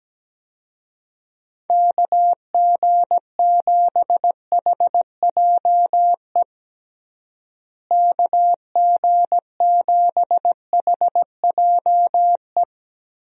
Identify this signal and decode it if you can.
Morse code, heard as a keyed tone, about 17 wpm, 700 Hz. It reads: KG7HJE KG7HJE
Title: USA Call Signs - 6 Characters - Repeated 3x - 17wpm